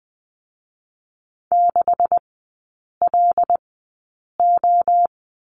Morse code, heard as a keyed tone, about 20 wpm, spaced as for 8 wpm, 690 Hz. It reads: 6LO
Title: 159 Lesson 26e - Letter g - Three Characters 20wpm